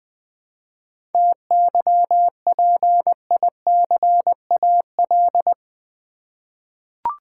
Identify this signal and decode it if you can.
Morse code, heard as a keyed tone, about 20 wpm, 700 Hz. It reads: TYPICAL
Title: Sentences from Top 1100 Words - New Word Drill Alt v2 - 20wpm